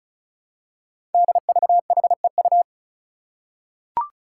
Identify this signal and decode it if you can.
Morse code, heard as a keyed tone, about 36 wpm, 700 Hz. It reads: DVHEU
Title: Five Challenging Letter-Number Combinations 35wpm